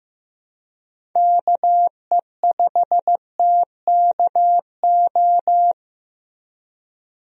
Morse code, heard as a keyed tone, about 15 wpm, 700 Hz. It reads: KE5TKO
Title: USA Call Signs - 6 Characters - Repeated 2x - 15wpm